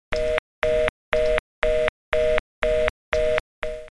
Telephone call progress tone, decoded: reorder tone